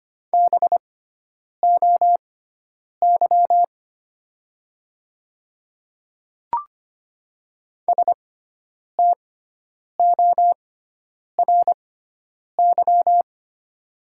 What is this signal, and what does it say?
Morse code, heard as a keyed tone, about 25 wpm, spaced as for 9 wpm, 700 Hz. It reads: BOY STORY
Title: 6x Character Spacing - Top 500 Words - 25wpm